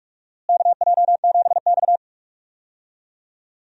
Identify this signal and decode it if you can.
Morse code, heard as a keyed tone, about 46 wpm, 700 Hz. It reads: KJ7X
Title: USA Call Signs - 4 Characters - Repeated 3x - 45wpm